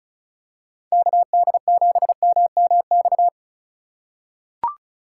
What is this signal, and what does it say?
Morse code, heard as a keyed tone, about 34 wpm, 700 Hz. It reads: KD7MMX
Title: USA Call Signs - 2x Character Spacing - 35wpm